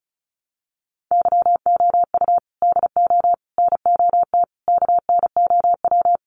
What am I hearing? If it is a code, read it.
Morse code, heard as a keyed tone, about 34 wpm, 700 Hz. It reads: YOU DO NOT KNOW
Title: Sentences from Top 600 Words 35wpm